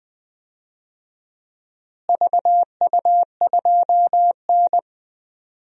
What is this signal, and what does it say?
Morse code, heard as a keyed tone, about 21 wpm, 700 Hz. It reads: VU2N
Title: DXCC-100 Call Signs - 4 Characters - Repeated 3x - 20wpm